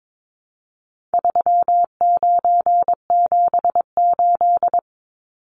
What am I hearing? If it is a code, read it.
Morse code, heard as a keyed tone, about 22 wpm, 700 Hz. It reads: 3978
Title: Numbers - 4-Digit 22wpm